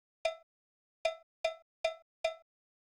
There is no code.